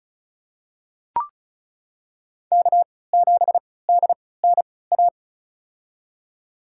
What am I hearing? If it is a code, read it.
Morse code, heard as a keyed tone, about 36 wpm, spaced as for 20 wpm, 700 Hz. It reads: K7DNA
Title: USA Call Signs - 5 Characters - 3x Character Spacing - 35wpm